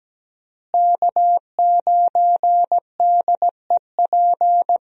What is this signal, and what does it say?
Morse code, heard as a keyed tone, about 17 wpm, 700 Hz. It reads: K9DEP
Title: USA Call Signs - 5 Characters - Repeated 3x - 17wpm